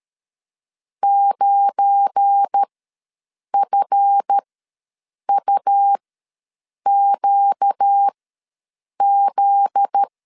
Morse code, heard as a keyed tone, about 12 wpm, spaced as for 7 wpm, 800 Hz. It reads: 9FUQZ
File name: LEZIONI / 9 2867 ws 12wpm 3431